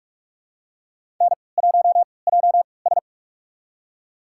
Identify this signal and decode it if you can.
Morse code, heard as a keyed tone, about 41 wpm, spaced as for 25 wpm, 700 Hz. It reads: N1JS
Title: USA Call Signs - 4 Characters - 3x Character Spacing - 45wpm